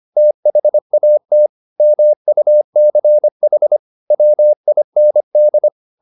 Morse code, heard as a keyed tone, about 25 wpm, 600 Hz. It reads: THAT MUCH WIND